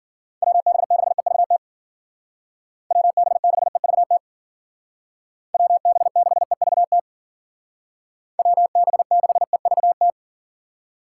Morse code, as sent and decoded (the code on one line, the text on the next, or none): .-- -... -.... ...- - / .-- -... -.... ...- - / .-- -... -.... ...- - / .-- -... -.... ...- -
WB6EVT WB6EVT WB6EVT WB6EVT